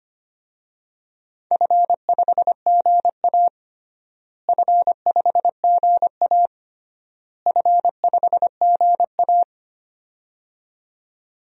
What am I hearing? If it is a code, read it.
Morse code, heard as a keyed tone, about 25 wpm, 700 Hz. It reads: F5GA F5GA F5GA
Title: DXCC-100 Call Signs - 4 Characters - Repeated 3x - 25wpm